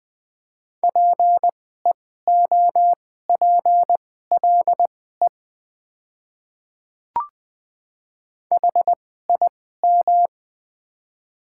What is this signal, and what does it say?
Morse code, heard as a keyed tone, about 20 wpm, spaced as for 14 wpm, 700 Hz. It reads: PEOPLE HIM
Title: 2x Character Spacing - Top 100 Words - 20wpm